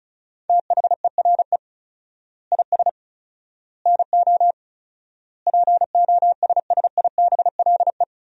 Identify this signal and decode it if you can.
Morse code, heard as a keyed tone, about 34 wpm, 700 Hz. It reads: THERE IS NO POSSIBLE
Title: 4x Word-Spacing - Sentences from Top 500 Words - 35wpm